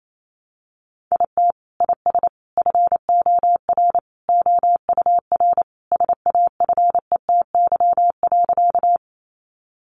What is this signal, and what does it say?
Morse code, heard as a keyed tone, about 28 wpm, 700 Hz. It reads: IT IS FOR OUR SAFETY.
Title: Sentences from Top 700 Words 28wpm